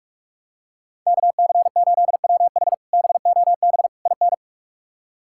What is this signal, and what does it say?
Morse code, heard as a keyed tone, about 44 wpm, 700 Hz. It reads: KC8WH BOB IN
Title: K1USN SST QSO Practice - 45wpm